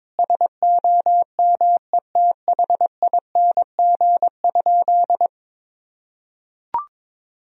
Morse code, heard as a keyed tone, about 22 wpm, 700 Hz. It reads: SOMETHING?
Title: Sentences from Top 1000 Words - Review Part 2 - 22wpm